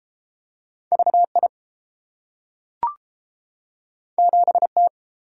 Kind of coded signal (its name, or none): Morse code